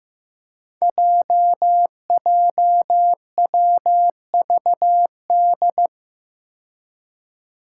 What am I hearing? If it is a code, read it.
Morse code, heard as a keyed tone, about 15 wpm, 700 Hz. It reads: JJWVD
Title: Code Groups - 5 Letters - 15wpm